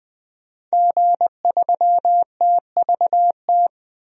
Morse code, heard as a keyed tone, about 20 wpm, 700 Hz. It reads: G3TVT
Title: DXCC-100 Call Signs - 5 Characters - 20wpm